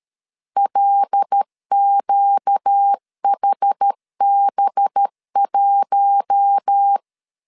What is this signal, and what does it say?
Morse code, heard as a keyed tone, about 13 wpm, 800 Hz. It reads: LQHB1
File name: CW / DET / 1 6 1500C S B 12wpm 3431